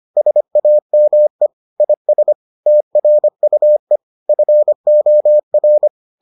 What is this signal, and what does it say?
Morse code, heard as a keyed tone, about 25 wpm, 600 Hz. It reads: SAME IS TRUE FOR